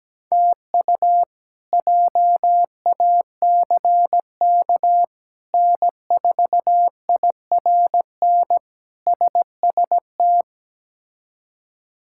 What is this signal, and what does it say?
Morse code, heard as a keyed tone, about 17 wpm, 700 Hz. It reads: TU JACK N4IRN SST
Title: K1USN SST QSO Practice - 17wpm